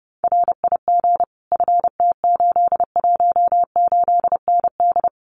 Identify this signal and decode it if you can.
Morse code, heard as a keyed tone, about 30 wpm, 700 Hz. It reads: RIG FT818ND
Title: Common QSO Phrases Encore 30wpm